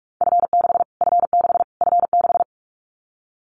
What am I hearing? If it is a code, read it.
Morse code, heard as a keyed tone, about 44 wpm, 700 Hz. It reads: F6 F6 F6
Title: Callsign Prefixes for Top 25 DXCC - Repeated 3x - 45wpm